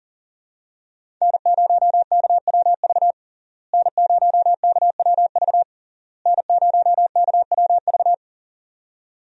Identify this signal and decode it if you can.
Morse code, heard as a keyed tone, about 40 wpm, 700 Hz. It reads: N0KWV N0KWV N0KWV
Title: USA Call Signs - 5 Characters - Repeated 3x - 40wpm